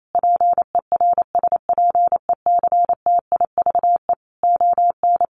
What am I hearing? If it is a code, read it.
Morse code, heard as a keyed tone, about 28 wpm, 700 Hz. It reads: PERSPECTIVE ON
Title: Sentences from Top 1200 Words 28wpm